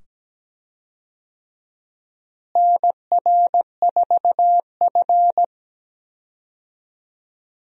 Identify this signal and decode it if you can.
Morse code, heard as a keyed tone, about 17 wpm, 700 Hz. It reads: NR4F
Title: USA Call Signs - 4 Characters - Repeated 2x - 17wpm